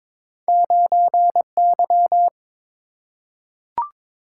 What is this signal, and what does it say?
Morse code, heard as a keyed tone, about 22 wpm, 700 Hz. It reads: 9Y
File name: practice / 22wpm / Two Letter-Number Combinations - 22wpm